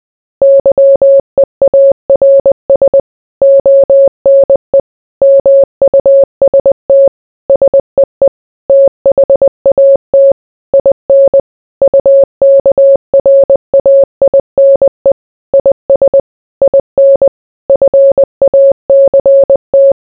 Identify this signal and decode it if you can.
Morse code, heard as a keyed tone, about 20 wpm, 550 Hz. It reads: YEARS ONE MUST SEE THAT IN UKRAINE IS IN FACT